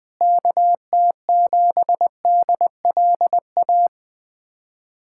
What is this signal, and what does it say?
Morse code, heard as a keyed tone, about 20 wpm, 700 Hz. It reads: KT7DLA
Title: USA Callsign DE Callsign 20wpm